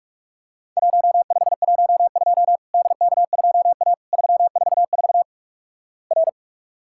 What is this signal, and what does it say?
Morse code, heard as a keyed tone, about 46 wpm, 700 Hz. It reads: 1512 DK2A 344 R
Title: Worked All Europe DX Contest - QTC Exchange - 45wpm